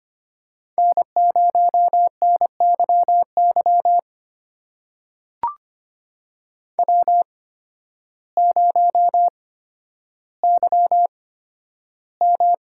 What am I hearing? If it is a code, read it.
Morse code, heard as a keyed tone, about 25 wpm, spaced as for 12 wpm, 700 Hz. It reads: N0NYY W0YM
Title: USA Call Signs - 8x Character Spacing - 25wpm